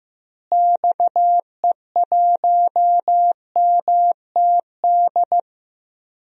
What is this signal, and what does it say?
Morse code, heard as a keyed tone, about 16 wpm, 700 Hz. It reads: XE1MTD
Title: DXCC-100 Call Signs - 6 Characters - Repeated 2x - 15wpm